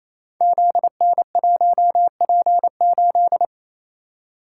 Morse code, heard as a keyed tone, about 27 wpm, 700 Hz. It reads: ZN1P8
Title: Code Groups - 5 Letter:Numbers - 28wpm